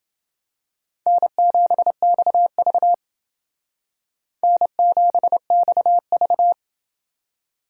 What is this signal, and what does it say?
Morse code, heard as a keyed tone, about 29 wpm, 700 Hz. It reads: N7XV N7XV